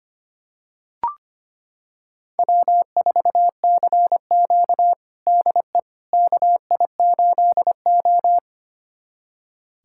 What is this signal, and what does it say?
Morse code, heard as a keyed tone, about 25 wpm, 700 Hz. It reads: W4CQ DE KI8O
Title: USA Callsign DE Callsign - 4 Characters - 25wpm